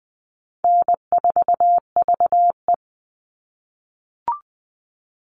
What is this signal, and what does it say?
Morse code, heard as a keyed tone, about 20 wpm, 700 Hz. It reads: N4VE
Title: All Cut Numbers - 4-Digit 20wpm